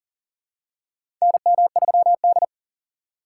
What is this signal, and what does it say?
Morse code, heard as a keyed tone, about 40 wpm, 700 Hz. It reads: NM3D
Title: USA Call Signs - 4 Characters - 4x Character Spacing - 40wpm